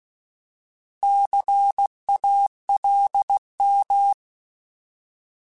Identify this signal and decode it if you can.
Morse code, heard as a keyed tone, about 16 wpm, 800 Hz. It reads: CALM
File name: Alphabetical / Alphabetical mp3s / Cs @15